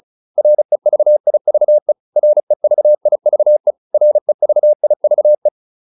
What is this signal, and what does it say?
Morse code, heard as a keyed tone, about 34 wpm, 600 Hz. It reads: REVIVE REVIVE REVIVE
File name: RE6 35